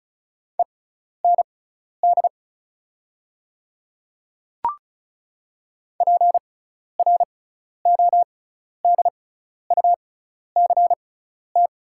Morse code, heard as a keyed tone, about 36 wpm, spaced as for 12 wpm, 690 Hz. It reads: END PRODUCT
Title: 6x Character Spacing - Top 500 Words - 35wpm